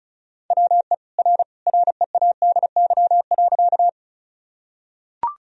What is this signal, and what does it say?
Morse code, heard as a keyed tone, about 36 wpm, 700 Hz. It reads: WE R READY.
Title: Sentences from Top 400 Words and Abbreviations 35wpm